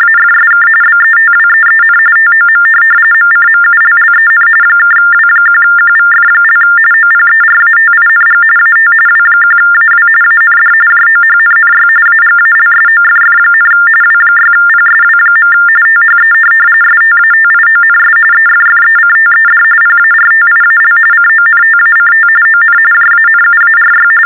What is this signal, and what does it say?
TTY/TDD text telephone tones, about 45.45 baud. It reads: AFSK BETWEEN TOM MCMULLEN (W1QVF) OPERATING AT W1AW AND JOHNNY AGALSOFF, W6PSW. THE STATIONS EFFECTED PARTIAL CONTACT ON JANUARY
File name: teletype-RTTY-45